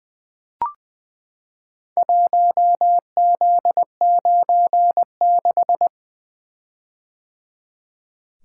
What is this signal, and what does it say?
Morse code, heard as a keyed tone, about 20 wpm, 700 Hz. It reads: 1Z96
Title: Four Letter-Number Combinations - 20wpm